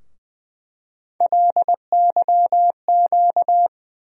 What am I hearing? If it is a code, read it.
Morse code, heard as a keyed tone, about 20 wpm, 700 Hz. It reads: LYQ
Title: USA Call Sign Postfixes - 3 Letters 20wpm